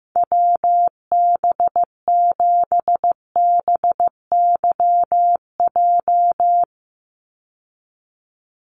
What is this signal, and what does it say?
Morse code, heard as a keyed tone, about 15 wpm, 700 Hz. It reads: WB7BYJ
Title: USA Call Signs 15wpm